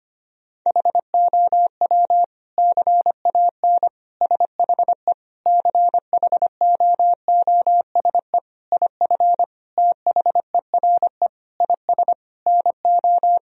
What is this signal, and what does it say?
Morse code, heard as a keyed tone, about 25 wpm, 700 Hz. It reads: HOW CAN SHE CHOOSE IF THERE IS NO
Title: Sentences from Top 1000 Words - Review Part 4 - 25wpm